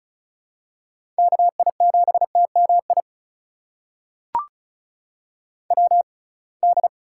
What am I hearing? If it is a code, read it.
Morse code, heard as a keyed tone, about 34 wpm, spaced as for 26 wpm, 700 Hz. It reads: KI7TMI WD
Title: USA Call Signs - 6 Characters - 6x Character Spacing - 35wpm